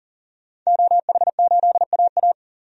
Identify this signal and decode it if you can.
Morse code, heard as a keyed tone, about 40 wpm, 700 Hz. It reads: OH8AA
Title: DXCC-100 Call Signs - 5 Characters - 40wpm